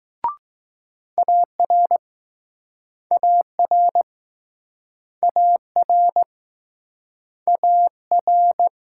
Morse code, heard as a keyed tone, about 20 wpm, 700 Hz. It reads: AR AR AR AR